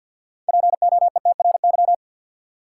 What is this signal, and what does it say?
Morse code, heard as a keyed tone, about 46 wpm, 700 Hz. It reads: POETRY
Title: Six Letter Words - 50wpm